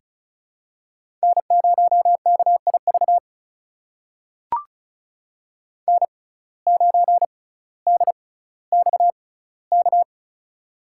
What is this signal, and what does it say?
Morse code, heard as a keyed tone, about 36 wpm, spaced as for 18 wpm, 700 Hz. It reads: N0KIV N9DXK
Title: USA Call Signs - 5 Characters - 6x Character Spacing - 35wpm